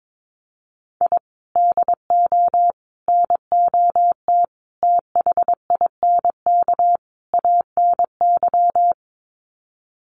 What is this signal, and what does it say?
Morse code, heard as a keyed tone, about 22 wpm, 700 Hz. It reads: I DO NOT THINK ANY
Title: Sets of 5 Words Encore 22wpm